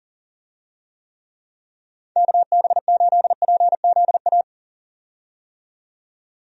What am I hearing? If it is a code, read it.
Morse code, heard as a keyed tone, about 40 wpm, 700 Hz. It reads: KB8PZA
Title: USA Call Signs - 6 Characters - 40wpm